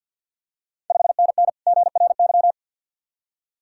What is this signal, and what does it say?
Morse code, heard as a keyed tone, about 51 wpm, 690 Hz. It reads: 5NN GRY
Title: Ontario QSO Party Exchange 50wpm